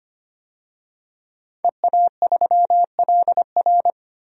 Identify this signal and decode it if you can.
Morse code, heard as a keyed tone, about 25 wpm, 700 Hz. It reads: EA3LR
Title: DXCC-100 Call Signs - 5 Characters - 25wpm